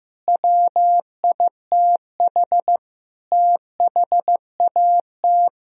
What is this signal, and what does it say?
Morse code, heard as a keyed tone, about 15 wpm, 700 Hz. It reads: WITH THAT
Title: Sentences from Top 1000 Words - Review Part 4 - 15wpm